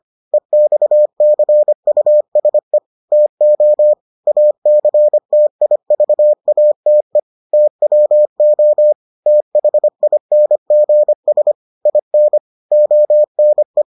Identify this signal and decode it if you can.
Morse code, heard as a keyed tone, about 25 wpm, 600 Hz. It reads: EXCUSE TO ACTIVATE TWO THINGS IN ONE